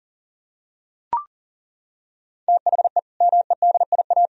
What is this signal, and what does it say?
Morse code, heard as a keyed tone, about 40 wpm, 700 Hz. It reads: THE MEDIA
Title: Sentences from Top 1000 Words - Review Part 4 - 40wpm